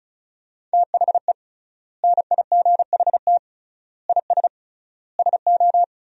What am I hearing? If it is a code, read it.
Morse code, heard as a keyed tone, about 36 wpm, 700 Hz. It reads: THE NIGHT IS SO
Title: 3x Word-Spacing - Sentences from Top 500 Words - 35wpm